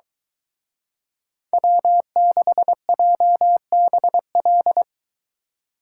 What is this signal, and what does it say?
Morse code, heard as a keyed tone, about 23 wpm, 700 Hz. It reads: W6JBL